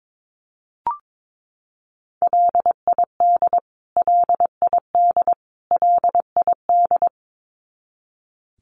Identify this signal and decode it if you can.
Morse code, heard as a keyed tone, about 22 wpm, 700 Hz. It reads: LID LID LID
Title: Abbreviations Repeated 3x - 22wpm